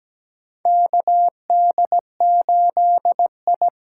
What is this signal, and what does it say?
Morse code, heard as a keyed tone, about 17 wpm, 700 Hz. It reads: KD8I
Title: USA Call Signs - 8x Character Spacing - 17wpm